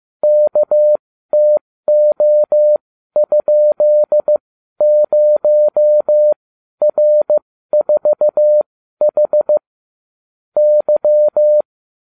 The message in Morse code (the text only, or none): KTO?0R4H Y